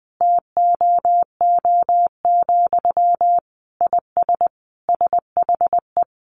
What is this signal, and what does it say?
Morse code, heard as a keyed tone, about 20 wpm, 700 Hz. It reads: TOO, IS SHE
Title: Sentences from Top 1300 Words 20wpm